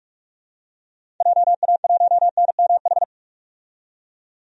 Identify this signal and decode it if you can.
Morse code, heard as a keyed tone, about 46 wpm, 700 Hz. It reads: JA1NMH
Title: DXCC-100 Call Signs - 6 Characters - 45wpm